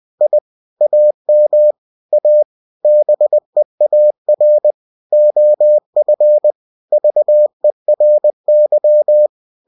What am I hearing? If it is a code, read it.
Morse code, heard as a keyed tone, about 20 wpm, 600 Hz. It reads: I AM A BEAR OF VERY
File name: cw-books / winnie-the-pooh / winnie-20wpm-page-0029